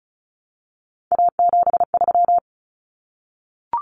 Morse code, heard as a keyed tone, about 34 wpm, 700 Hz. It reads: A73
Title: Common Cut Numbers - 3-Digit 35wpm